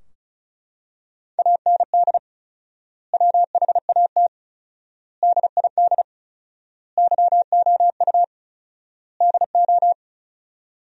Morse code, heard as a keyed tone, about 36 wpm, 700 Hz. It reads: AND WHAT DID YOU DO
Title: 4x Word-Spacing - Sentences from Top 100 Words - 35wpm